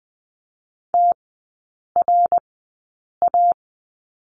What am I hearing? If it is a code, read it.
Morse code, heard as a keyed tone, about 20 wpm, spaced as for 8 wpm, 700 Hz. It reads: TRA
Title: 060 Lesson 11e - Letter h - Three Characters 20wpm